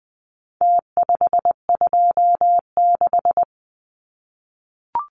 Morse code, heard as a keyed tone, about 20 wpm, 700 Hz. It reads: T526
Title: Numbers - Contest Style - 4-Digit 20wpm